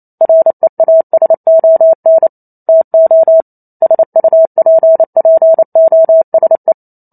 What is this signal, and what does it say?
Morse code, heard as a keyed tone, about 28 wpm, 650 Hz. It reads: REASON TO SUPPOSE
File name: Chapter 2 - 28 WPM